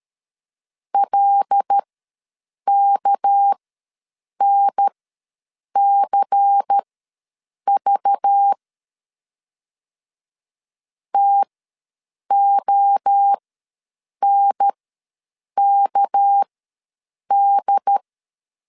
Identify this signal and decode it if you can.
Morse code, heard as a keyed tone, about 13 wpm, spaced as for 7 wpm, 800 Hz. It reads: LKNCV TONKD
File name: LEZIONI / 6 PKX0 / 6 PK ws 12wpm 3116